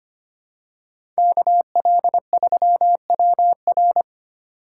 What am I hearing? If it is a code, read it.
Morse code, heard as a keyed tone, about 25 wpm, 700 Hz. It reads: KL3WR
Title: USA Call Signs - 5 Characters - 4x Character Spacing - 25wpm